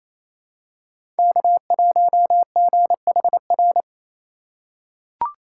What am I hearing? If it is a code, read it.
Morse code, heard as a keyed tone, about 28 wpm, 700 Hz. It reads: K1GHR
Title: USA Call Signs - 5 Characters - 2x Character Spacing - 28wpm